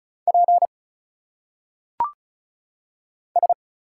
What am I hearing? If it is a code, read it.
Morse code, heard as a keyed tone, about 36 wpm, 700 Hz. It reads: P S
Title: Single Letters - Phonetic Pronunciation 35wpm